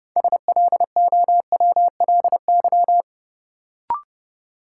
Morse code, heard as a keyed tone, about 30 wpm, 700 Hz. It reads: SLOWLY
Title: Sentences from Top 1100 Words - New Word Drill - 30wpm